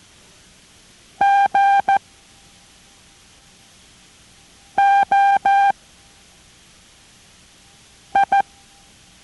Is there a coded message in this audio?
Yes: Morse code from 1.2 to 8.4 s, heard as a keyed tone, about 14 wpm, 790 Hz. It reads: G O I